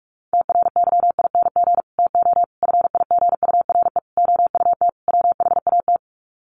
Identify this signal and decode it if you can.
Morse code, heard as a keyed tone, about 44 wpm, 700 Hz. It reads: TRYING TO FIGURE OUT WHAT
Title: Sets of 5 Words 45wpm